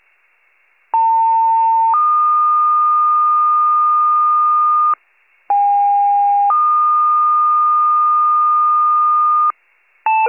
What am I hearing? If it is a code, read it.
Two-tone page, heard as a one-second tone, then a three-second tone, about 900 then 1220 Hz, about 810 then 1220 Hz. It is heard from 0.9 to 9.5 s.
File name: pager / 78 RESCUE / wcalert